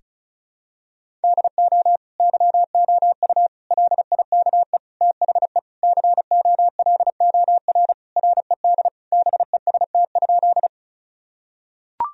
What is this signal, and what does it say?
Morse code, heard as a keyed tone, about 34 wpm, 700 Hz. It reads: DO YOU LIKE THE COLOR RED BEST?
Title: Sentences from Top 1000 Words - Review Part 2 - 35wpm